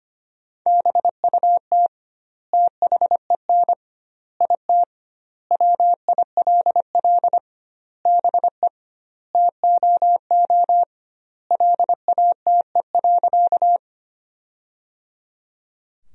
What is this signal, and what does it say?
Morse code, heard as a keyed tone, about 25 wpm, 700 Hz. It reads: BUT THEN IT WILL BE TOO LATE.
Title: 2x Word-Spacing - Sentences from Top 300 Words - 25wpm